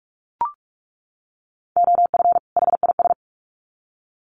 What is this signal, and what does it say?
Morse code, heard as a keyed tone, about 44 wpm, 700 Hz. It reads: OF HIS
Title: Sets of 2 Words 45wpm